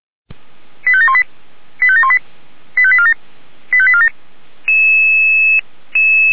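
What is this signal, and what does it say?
Five-tone selective call (ZVEI), heard as quick runs of five tones, tones of about 70 ms. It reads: 85718 85718 85847 85847
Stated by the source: ZVEI five-tone call received by SDR.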